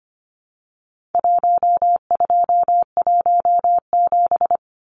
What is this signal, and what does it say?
Morse code, heard as a keyed tone, about 25 wpm, 700 Hz. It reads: 1217
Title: Numbers - Contest Style - 4-Digit 25wpm